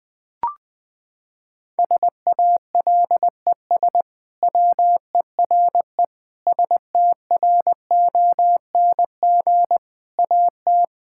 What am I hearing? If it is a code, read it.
Morse code, heard as a keyed tone, about 20 wpm, 700 Hz. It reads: SALES WERE STRONG AT
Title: Sentences with 6-Word Phrases - 20wpm